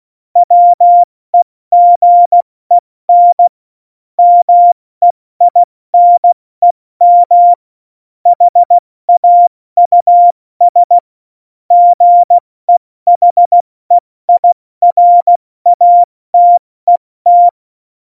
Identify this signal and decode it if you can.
Morse code, heard as a keyed tone, about 16 wpm, 700 Hz. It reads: WEGEN MEINEM HAUS GEHEIRATET